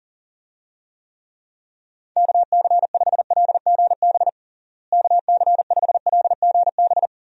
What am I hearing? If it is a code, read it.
Morse code, heard as a keyed tone, about 37 wpm, 700 Hz. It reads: KC5LGB KC5LGB
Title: USA Call Signs - 6 Characters - Repeated 3x - 40wpm